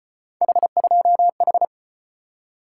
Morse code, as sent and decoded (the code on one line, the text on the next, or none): .... ..--- ....
H2H